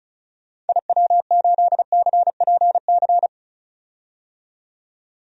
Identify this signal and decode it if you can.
Morse code, heard as a keyed tone, about 36 wpm, 700 Hz. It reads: IW8CPC
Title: DXCC-100 Call Signs - 6 Characters - 35wpm